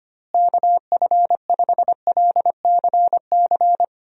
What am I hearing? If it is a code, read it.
Morse code, heard as a keyed tone, about 25 wpm, 700 Hz. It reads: KF5LCC
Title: USA Call Signs - 6 Characters - Repeated 2x - 25wpm